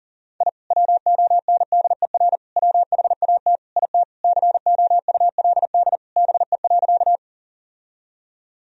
Morse code, heard as a keyed tone, about 37 wpm, 700 Hz. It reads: I WONDER WHAT IT COULD BE.
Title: Sentences from Top 1000 Words - Review Part 3 - 40wpm